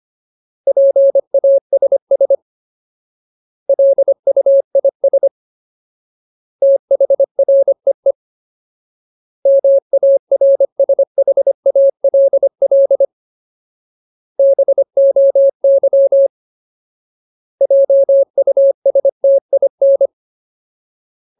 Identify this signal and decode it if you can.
Morse code, heard as a keyed tone, about 25 wpm, 550 Hz. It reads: PASS LUIS THREE MARSHALL BOY JUSTIN
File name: Ultimate Word List / 25WPM Ultimate